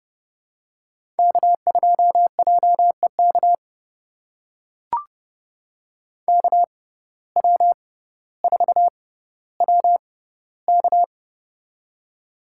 Morse code, heard as a keyed tone, about 30 wpm, spaced as for 15 wpm, 700 Hz. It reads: K2JEK KW4WK
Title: USA Call Signs - 6x Character Spacing - 30wpm